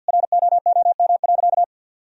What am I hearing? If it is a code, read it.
Morse code, heard as a keyed tone, about 52 wpm, 700 Hz. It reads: ROOM.
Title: Sentences from Top 1000 Words - Review Part 1 - 50wpm